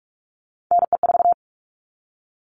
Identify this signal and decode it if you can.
Morse code, heard as a keyed tone, about 44 wpm, 700 Hz. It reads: NE4
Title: All Cut Numbers - 3-Digit 45wpm